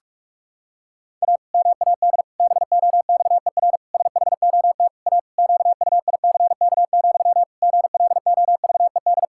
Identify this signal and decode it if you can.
Morse code, heard as a keyed tone, about 41 wpm, 700 Hz. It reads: A MAD BOXER SHOT A QUICK, GLOVED
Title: Pangram Challenge v2 - 45wpm